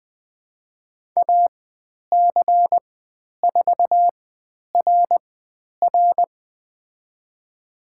Morse code, heard as a keyed tone, about 20 wpm, spaced as for 10 wpm, 700 Hz. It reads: AC4RR
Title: 095 Lesson 16g - Letter c - Call Signs 20wpm